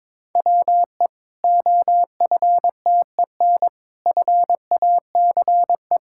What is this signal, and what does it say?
Morse code, heard as a keyed tone, about 22 wpm, 700 Hz. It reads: WE OFTEN FACE
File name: Difficult Sentences with Hints - 22wpm